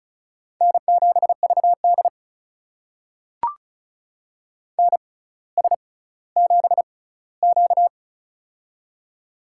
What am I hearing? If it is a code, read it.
Morse code, heard as a keyed tone, about 36 wpm, spaced as for 18 wpm, 700 Hz. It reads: N7VD NS7Q
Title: USA Call Signs - 4 Characters - 6x Character Spacing - 35wpm